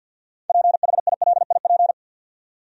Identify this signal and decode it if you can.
Morse code, heard as a keyed tone, about 53 wpm, 700 Hz. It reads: PHILIP
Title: USA Common Names - 50wpm